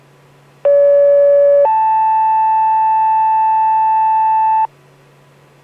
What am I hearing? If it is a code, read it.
Two-tone page, heard as a one-second tone, then a three-second tone, about 570 then 880 Hz, with noise. It is heard from 0.6 to 4.7 s.